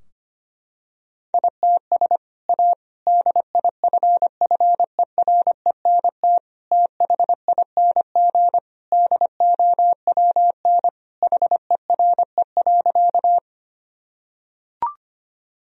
Morse code, heard as a keyed tone, about 25 wpm, 700 Hz. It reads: ITS A DIFFERENT THING DOWN HERE.
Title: Sentences from Top 1000 Words - Review Part 3 - 25wpm